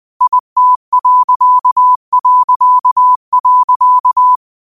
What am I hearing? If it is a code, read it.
Morse code, heard as a keyed tone, about 20 wpm, 1000 Hz. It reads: IT...
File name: qotdfiles / 20wpm